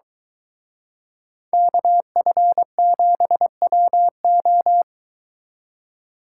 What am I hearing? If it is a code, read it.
Morse code, heard as a keyed tone, about 23 wpm, 700 Hz. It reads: KF7WO